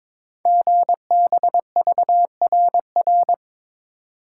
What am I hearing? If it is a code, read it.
Morse code, heard as a keyed tone, about 22 wpm, 700 Hz. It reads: GBVRR
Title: Code Groups - 5 Letters - 22wpm